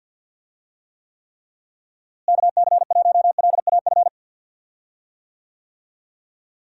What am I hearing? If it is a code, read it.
Morse code, heard as a keyed tone, about 53 wpm, 700 Hz. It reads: KC1LAF
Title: USA Call Signs - 6 Characters - Repeated 3x - 50wpm